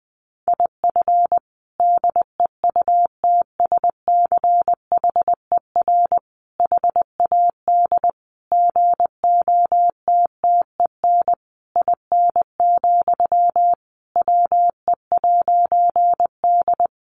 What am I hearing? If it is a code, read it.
Morse code, heard as a keyed tone, about 20 wpm, 700 Hz. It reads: IF DEUTSCHER HAD GOTTEN IN, WE'D
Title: A Sound Of Thunder 20wpm